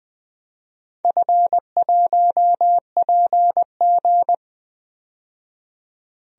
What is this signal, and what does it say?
Morse code, heard as a keyed tone, about 20 wpm, 700 Hz. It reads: F1PG
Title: DXCC-100 Call Signs - 4 Characters - 20wpm